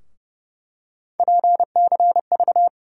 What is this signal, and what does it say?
Morse code, heard as a keyed tone, about 30 wpm, 700 Hz. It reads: PCV
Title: USA Call Sign Postfixes - 3 Letters 30wpm